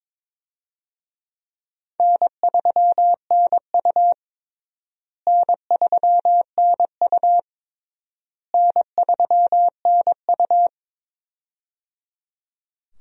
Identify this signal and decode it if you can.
Morse code, heard as a keyed tone, about 22 wpm, 700 Hz. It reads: N3NU N3NU N3NU
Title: USA Call Signs - 4 Characters - Repeated 3x - 22wpm